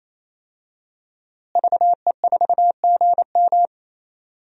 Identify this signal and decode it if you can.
Morse code, heard as a keyed tone, about 28 wpm, 700 Hz. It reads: VE4GM
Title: DXCC-100 Call Signs - 5 Characters - Repeated 3x - 28wpm